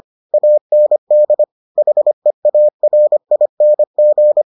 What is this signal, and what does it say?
Morse code, heard as a keyed tone, about 25 wpm, 600 Hz. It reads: AND HEARING